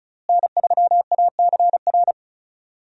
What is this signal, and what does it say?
Morse code, heard as a keyed tone, about 34 wpm, 700 Hz. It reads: N3ACR
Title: USA Call Signs - 5 Characters - Repeated 3x - 35wpm